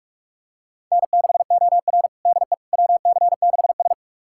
Morse code, heard as a keyed tone, about 41 wpm, 700 Hz. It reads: N6OR DE WC6S
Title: USA Callsign DE Callsign - 4 Characters - 45wpm